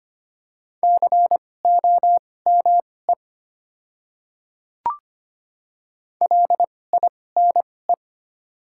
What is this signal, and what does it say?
Morse code, heard as a keyed tone, about 25 wpm, spaced as for 18 wpm, 700 Hz. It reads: COME LINE
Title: 2x Character Spacing - Top 200 Words - 25wpm